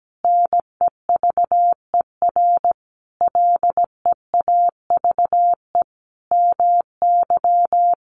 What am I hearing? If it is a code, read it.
Morse code, heard as a keyed tone, about 17 wpm, 700 Hz. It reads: NEVER LEAVE MY
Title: Sentences from Top 700 Words 17wpm